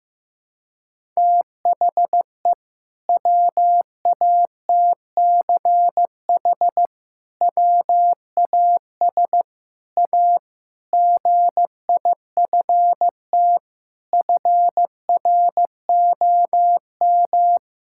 Morse code, heard as a keyed tone, about 15 wpm, 700 Hz. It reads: THE WATCH WAS A GIFT FROM
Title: Sentences from Top 1000 Words - Review Part 6 - 15wpm